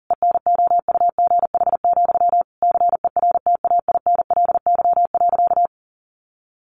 Morse code, heard as a keyed tone, about 40 wpm, 700 Hz. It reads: ENOUGH, CERTAINLY.
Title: Sentences from Top 900 Words 40wpm